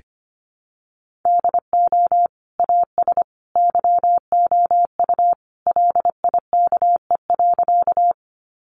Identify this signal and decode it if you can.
Morse code, heard as a keyed tone, about 25 wpm, 700 Hz. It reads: DO AS YOU LIKE.
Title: Sentences from Top 100 Words 25wpm